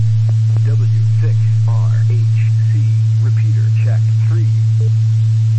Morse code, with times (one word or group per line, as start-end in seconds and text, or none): none